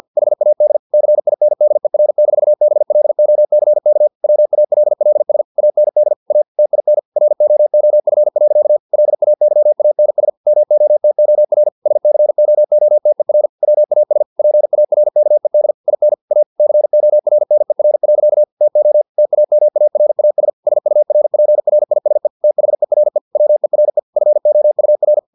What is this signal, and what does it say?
Morse code, heard as a keyed tone, about 50 wpm, 600 Hz. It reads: HAD CINDER-BLOCK WALLS AND A TIN ROOF. LAYANS MOTOR SCOOTER WAS PARKED IN A CORNER. TO TAMARAS SURPRISE THERE WERE FOUR
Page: Never 49wpm